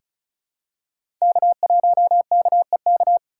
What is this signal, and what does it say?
Morse code, heard as a keyed tone, about 36 wpm, 700 Hz. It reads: K1KEK
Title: USA Call Signs - 3x Character Spacing - 35wpm